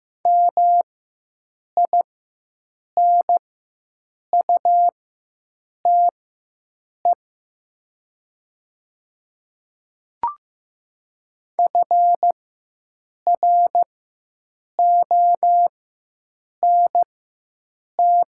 Morse code, heard as a keyed tone, about 15 wpm, spaced as for 7 wpm, 700 Hz. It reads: MINUTE FRONT
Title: 4x Character Spacing - Top 500 Words - 15wpm